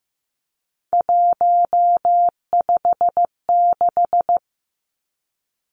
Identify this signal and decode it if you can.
Morse code, heard as a keyed tone, about 15 wpm, 700 Hz. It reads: 156
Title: Numbers - 3-Digit 15wpm